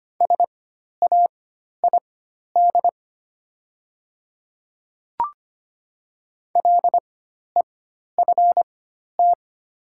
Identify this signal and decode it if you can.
Morse code, heard as a keyed tone, about 25 wpm, spaced as for 12 wpm, 700 Hz. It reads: SAID LEFT
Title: 4x Character Spacing - Top 300 Words - 25wpm